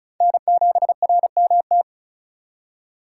Morse code, heard as a keyed tone, about 34 wpm, 700 Hz. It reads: N7RMT
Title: USA Call Signs - 5 Characters - 2x Character Spacing - 35wpm